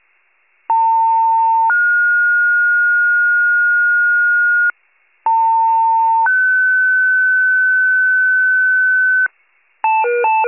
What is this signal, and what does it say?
Two-tone page, heard as a one-second tone, then a three-second tone, about 900 then 1430 Hz, about 900 then 1500 Hz. It is heard from 0.7 to 9.3 s.
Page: /wav/pager/archive/78 FIRE/